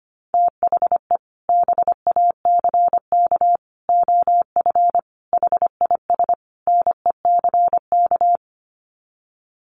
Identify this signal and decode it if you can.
Morse code, heard as a keyed tone, about 25 wpm, 700 Hz. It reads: THE BACK OF HIS NECK
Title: Sets of 5 Words Encore 25wpm